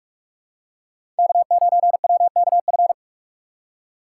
Morse code, heard as a keyed tone, about 42 wpm, 700 Hz. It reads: K9WKF
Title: USA Call Signs - 5 Characters - 2x Character Spacing - 45wpm